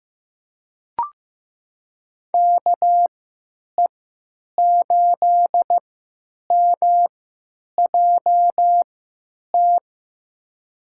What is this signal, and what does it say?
Morse code, heard as a keyed tone, about 15 wpm, spaced as for 9 wpm, 700 Hz. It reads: KE8MJT
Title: USA Call Signs - 6 Characters - 3x Character Spacing - 15wpm